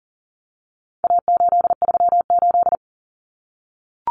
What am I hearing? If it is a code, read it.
Morse code, heard as a keyed tone, about 40 wpm, 700 Hz. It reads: A838
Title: Common Cut Numbers - 4-Digit 40wpm